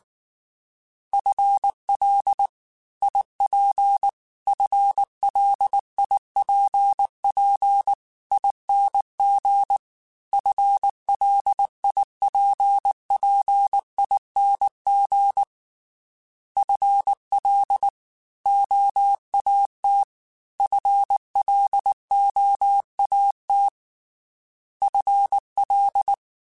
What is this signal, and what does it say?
Morse code, heard as a keyed tone, about 19 wpm, 800 Hz. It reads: FL IP FLIPP ING FLIPPING FL OAT FLOAT FL
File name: FL / FL mp3s / Fast FL @18